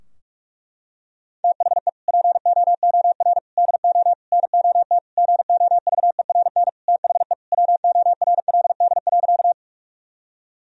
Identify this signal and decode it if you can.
Morse code, heard as a keyed tone, about 46 wpm, 700 Hz. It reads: THE POOR DO NOT GOVERN THE WORLD.
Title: Sentences from Top 1000 Words - Review Part 2 - 45wpm